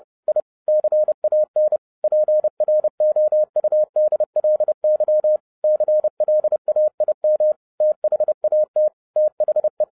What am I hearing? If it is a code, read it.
Morse code, heard as a keyed tone, about 30 wpm, 610 Hz. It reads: I CAN PROUDLY CLAIM THAT THE
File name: 2012-01-mp3 / n11-I DIDNT MISS THE SOLAR CYCLE 24-30wpm